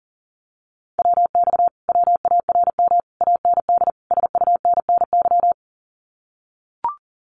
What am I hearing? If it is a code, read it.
Morse code, heard as a keyed tone, about 40 wpm, 700 Hz. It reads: WX WARM AND SUNNY
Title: Common QSO Phrases Encore 40wpm